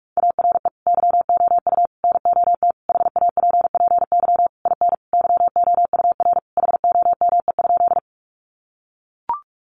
Morse code, heard as a keyed tone, about 44 wpm, 700 Hz. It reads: ARE YOU NOT HAPPY IN YOUR HOME?
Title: Sentences from Top 1000 Words 45wpm